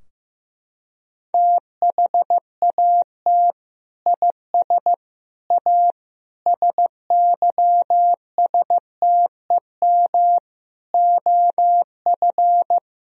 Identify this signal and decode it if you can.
Morse code, heard as a keyed tone, about 16 wpm, 700 Hz. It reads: THAT IS A SYSTEM OF
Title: Sentences from Top 1000 Words - Review Part 3 - 15wpm